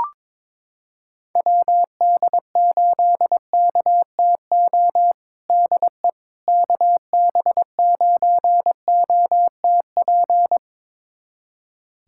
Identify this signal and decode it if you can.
Morse code, heard as a keyed tone, about 22 wpm, 700 Hz. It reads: WD8KTO DE KB9OTP
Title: USA Callsign DE Callsign - 6 Characters - 22wpm